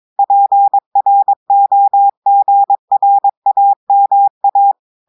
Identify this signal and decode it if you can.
Morse code, heard as a keyed tone, about 22 wpm, 800 Hz. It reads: PROGRAMA
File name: audios / portugais / portugais texte en clair 2 vitesse 22 mots minutes